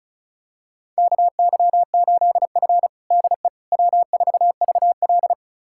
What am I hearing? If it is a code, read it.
Morse code, heard as a keyed tone, about 36 wpm, 690 Hz. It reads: KY8F DE W4VL
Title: USA Callsign DE Callsign - 4 Characters - 35wpm